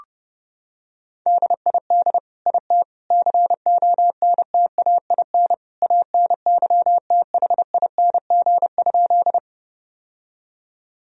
Morse code, heard as a keyed tone, about 30 wpm, 700 Hz. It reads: DID IT CONTAIN ANYTHING?
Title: Sentences from Top 1000 Words - Review Part 2 - 30wpm